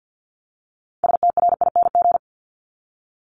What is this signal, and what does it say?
Morse code, heard as a keyed tone, about 50 wpm, 700 Hz. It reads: STRING
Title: Multi-letter Phonemes Repeated 3x 50wpm